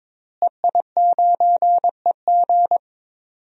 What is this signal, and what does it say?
Morse code, heard as a keyed tone, about 22 wpm, 700 Hz. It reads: EI9EG
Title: DXCC-100 Call Signs - 5 Characters - Repeated 3x - 22wpm